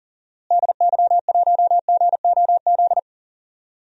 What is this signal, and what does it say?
Morse code, heard as a keyed tone, about 40 wpm, 700 Hz. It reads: DY1GOZ
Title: DXCC-100 Call Signs - 6 Characters - Repeated 2x - 40wpm